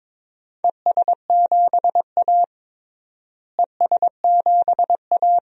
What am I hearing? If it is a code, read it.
Morse code, heard as a keyed tone, about 22 wpm, 700 Hz. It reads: ES7A ES7A
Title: DXCC-100 Call Signs - 4 Characters - Repeated 3x - 22wpm